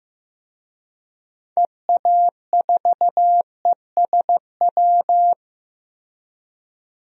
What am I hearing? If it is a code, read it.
Morse code, heard as a keyed tone, about 15 wpm, 700 Hz. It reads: EA4ESW
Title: DXCC-100 Call Signs - 6 Characters - Repeated 2x - 15wpm